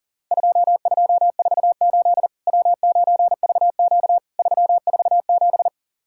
Morse code, heard as a keyed tone, about 37 wpm, 700 Hz. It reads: 2248 W9VQ 347
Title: Worked All Europe DX Contest - QTC Exchange - 40wpm